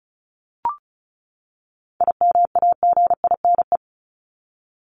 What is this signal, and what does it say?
Morse code, heard as a keyed tone, about 34 wpm, 700 Hz. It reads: IMAGINE
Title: Top 1251-1500 Words 35wpm